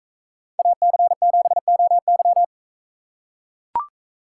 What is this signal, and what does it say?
Morse code, heard as a keyed tone, about 44 wpm, 700 Hz. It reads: AC7OY